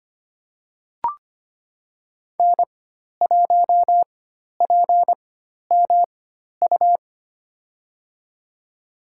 Morse code, heard as a keyed tone, about 25 wpm, spaced as for 12 wpm, 700 Hz. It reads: N1PMU
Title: USA Call Signs - 5 Characters - 4x Character Spacing - 25wpm